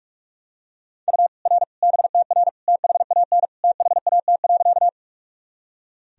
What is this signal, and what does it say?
Morse code, heard as a keyed tone, about 46 wpm, 700 Hz. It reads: U R BTR THAN THAT.
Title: Sentences from Top 400 Words and Abbreviations 45wpm